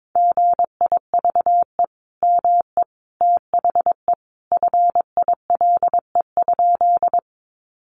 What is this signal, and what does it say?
Morse code, heard as a keyed tone, about 22 wpm, 700 Hz. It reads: GIVE ME THE FILE?
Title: Sentences from Top 1200 Words 22wpm